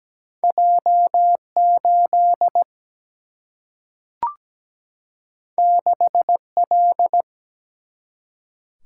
Morse code, heard as a keyed tone, about 17 wpm, 700 Hz. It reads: J 8 6 L
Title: Single-Letter Single-Number 17wpm